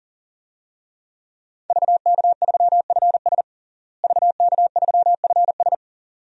Morse code, heard as a keyed tone, about 40 wpm, 700 Hz. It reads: VK3FS VK3FS
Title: USA Call Signs - 5 Characters - Repeated 2x - 40wpm